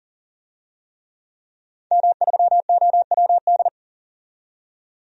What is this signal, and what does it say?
Morse code, heard as a keyed tone, about 38 wpm, 700 Hz. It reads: M3OWD
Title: DXCC-100 Call Signs - 5 Characters - 40wpm